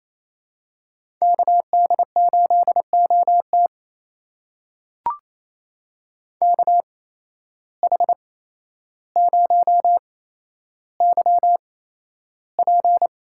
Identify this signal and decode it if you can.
Morse code, heard as a keyed tone, about 27 wpm, spaced as for 12 wpm, 700 Hz. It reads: KD8OT KH0YP
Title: USA Call Signs - 5 Characters - 8x Character Spacing - 28wpm